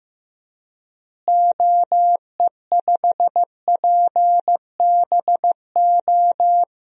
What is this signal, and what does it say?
Morse code, heard as a keyed tone, about 15 wpm, 700 Hz. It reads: OE5PBO
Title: DXCC-100 Call Signs - 6 Characters - Repeated 2x - 15wpm